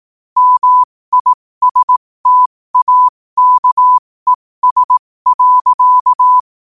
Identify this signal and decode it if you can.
Morse code, heard as a keyed tone, about 18 wpm, spaced as for 15 wpm, 1000 Hz. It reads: MISTAKES.
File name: qotdfiles / 15wpm